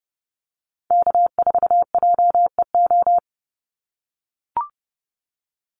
Morse code, heard as a keyed tone, about 30 wpm, 700 Hz. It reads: K4JEO
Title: USA Call Signs 30wpm